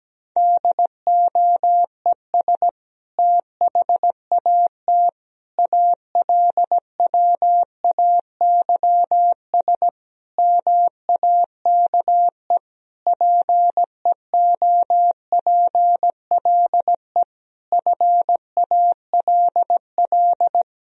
Morse code, heard as a keyed tone, about 17 wpm, 700 Hz. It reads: DOES THAT ALWAYS MAKE PEOPLE FALL
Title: Sentences from Top 1000 Words - Review Part 2 - 17wpm